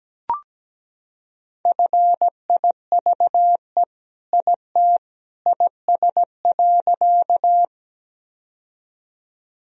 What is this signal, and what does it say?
Morse code, heard as a keyed tone, about 17 wpm, 700 Hz. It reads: FIVE IT IS.
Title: Sentences from Top 1000 Words - Review Part 2 - 17wpm